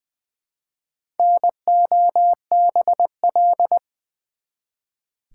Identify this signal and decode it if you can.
Morse code, heard as a keyed tone, about 20 wpm, 700 Hz. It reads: NOBL
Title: Build A Word - 1 Letter At A Time - 3 to 5-Letter Words - 20wpm